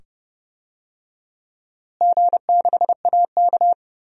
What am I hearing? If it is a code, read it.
Morse code, heard as a keyed tone, about 30 wpm, 700 Hz. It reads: G6AK
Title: DXCC-100 Call Signs - 4 Characters - Repeated 2x - 30wpm